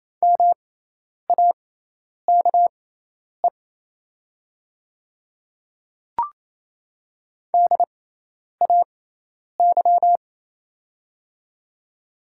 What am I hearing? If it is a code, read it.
Morse code, heard as a keyed tone, about 28 wpm, spaced as for 10 wpm, 700 Hz. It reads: MAKE DAY
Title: 6x Character Spacing - Top 200 Words - 28wpm